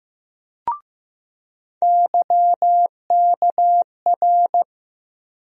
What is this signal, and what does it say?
Morse code, heard as a keyed tone, about 15 wpm, 700 Hz. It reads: YKR
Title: USA Call Sign Postfixes - 3 Letters 15wpm